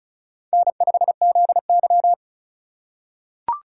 Morse code, heard as a keyed tone, about 36 wpm, 700 Hz. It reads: N5ZY
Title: USA Call Signs - 4 Characters - 4x Character Spacing - 35wpm